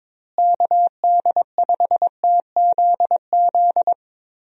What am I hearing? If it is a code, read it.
Morse code, heard as a keyed tone, about 22 wpm, 700 Hz. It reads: KD5TZZ
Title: USA Call Signs - 6 Characters - Repeated 2x - 22wpm